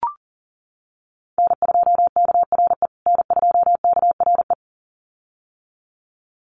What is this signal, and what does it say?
Morse code, heard as a keyed tone, about 40 wpm, 700 Hz. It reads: N2KRE N2KRE
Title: USA Call Signs Repeated 2x 40wpm